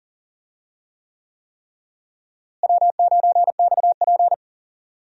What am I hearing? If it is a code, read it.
Morse code, heard as a keyed tone, about 40 wpm, 700 Hz. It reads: W9XP
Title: USA Call Signs - 4 Characters - Repeated 3x - 40wpm